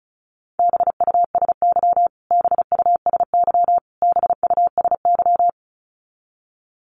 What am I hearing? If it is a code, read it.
Morse code, heard as a keyed tone, about 34 wpm, 700 Hz. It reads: BUSY BUSY BUSY
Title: Top 500 Words Repeated 35wpm